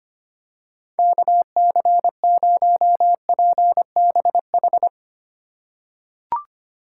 Morse code, heard as a keyed tone, about 25 wpm, 700 Hz. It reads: KC0PBH
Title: USA Call Signs - 6 Characters - 2x Character Spacing - 25wpm